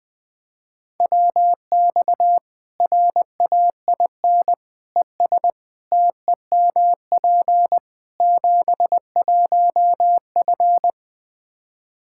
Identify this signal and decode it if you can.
Morse code, heard as a keyed tone, about 20 wpm, 700 Hz. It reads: WX RAIN ES TEMP 71F
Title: K1USN SST QSO Practice 20wpm